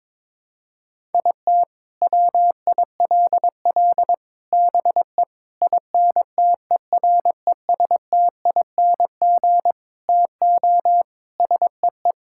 Morse code, heard as a keyed tone, about 22 wpm, 700 Hz. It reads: IT WILL BE INTERESTING TO SEE
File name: practice / 22wpm / Sets of 8 Words - 22wpm